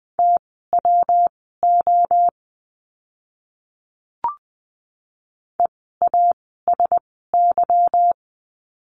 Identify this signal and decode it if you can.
Morse code, heard as a keyed tone, about 20 wpm, spaced as for 15 wpm, 700 Hz. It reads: TWO EASY
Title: 146 Lesson 24f - Letter y - Words 20wpm